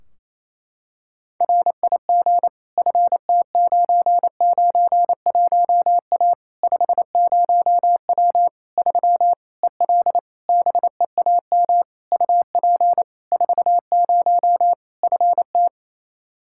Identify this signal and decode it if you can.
Morse code, heard as a keyed tone, about 28 wpm, 700 Hz. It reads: RIG FT991A 50W 3 EL BEAM UP 40 FT
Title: Practice QSO - Exchange 2 - Info - 28wpm